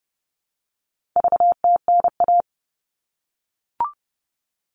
Morse code, heard as a keyed tone, about 30 wpm, 700 Hz. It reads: VTNA
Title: All Cut Numbers - 4-Digit 30wpm